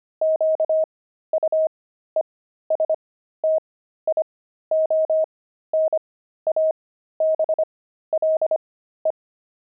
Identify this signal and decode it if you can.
Morse code, heard as a keyed tone, about 25 wpm, spaced as for 13 wpm, 630 Hz. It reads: QUESTIONABLE
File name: pangram-fs-13-cs-25-friars